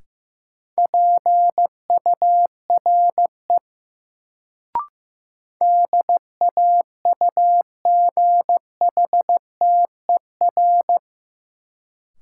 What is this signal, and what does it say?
Morse code, heard as a keyed tone, about 15 wpm, 700 Hz. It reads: PURE DAUGHTER
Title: Sentences from Top 1500 Words - New Word Drill - 15wpm